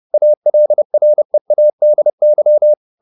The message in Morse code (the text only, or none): ALREADY